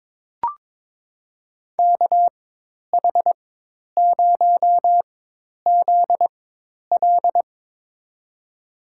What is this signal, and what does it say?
Morse code, heard as a keyed tone, about 22 wpm, spaced as for 10 wpm, 700 Hz. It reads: KH0ZL
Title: USA Call Signs - 5 Characters - 4x Character Spacing - 22wpm